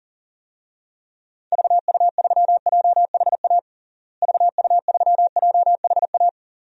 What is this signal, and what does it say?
Morse code, heard as a keyed tone, about 40 wpm, 700 Hz. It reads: VU3JHA VU3JHA
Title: DXCC-100 Call Signs - 6 Characters - Repeated 2x - 40wpm